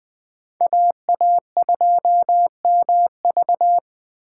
Morse code, heard as a keyed tone, about 20 wpm, 700 Hz. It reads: AA2MV
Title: Speed Racing - USA 2x2 Call Signs 17wpm